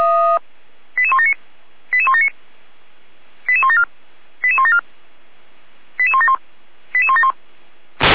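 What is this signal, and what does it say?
Five-tone selective call (ZVEI), heard as quick runs of five tones, tones of about 70 ms. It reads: 88179 88179 88174 88174 88171 88171
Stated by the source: ZVEI five-tone selective call received by SDR, from the audio server of Feuerwehr Orlamünde (fire brigade).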